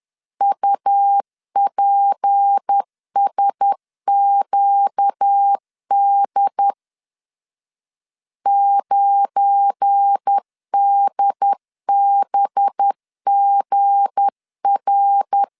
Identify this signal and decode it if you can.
Morse code, heard as a keyed tone, about 10 wpm, 800 Hz. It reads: UPSQD 9DBGR
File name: LEZIONI / 7 GF9Z / 7 GF9 10wpm 2747